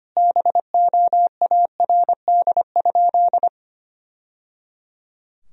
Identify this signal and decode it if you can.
Morse code, heard as a keyed tone, about 25 wpm, 700 Hz. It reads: BOARD?
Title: Sentences from Top 1000 Words - Review Part 4 - 25wpm